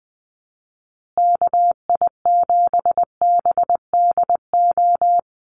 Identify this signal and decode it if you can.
Morse code, heard as a keyed tone, about 20 wpm, 700 Hz. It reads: KI7BDO
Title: USA Call Signs Repeated 2x 20wpm